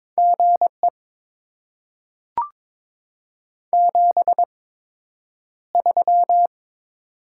Morse code, heard as a keyed tone, about 22 wpm, spaced as for 9 wpm, 700 Hz. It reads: GE 73
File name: Top 20 QSO Elements - 8x Character Spacing - 22wpm